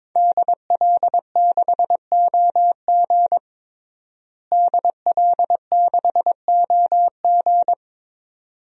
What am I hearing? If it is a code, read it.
Morse code, heard as a keyed tone, about 23 wpm, 700 Hz. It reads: DL6OG DL6OG
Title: DXCC-100 Call Signs - 5 Characters - Repeated 2x - 22wpm